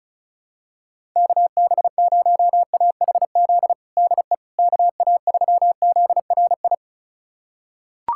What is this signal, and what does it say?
Morse code, heard as a keyed tone, about 36 wpm, 700 Hz. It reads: KB0AHZ DE KA3ZRI
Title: USA Callsign DE Callsign - 6 Characters - 35wpm